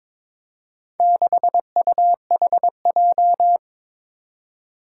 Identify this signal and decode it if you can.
Morse code, heard as a keyed tone, about 23 wpm, 700 Hz. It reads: 6UHJ
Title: Four Challenging Letter-Number Combinations 22wpm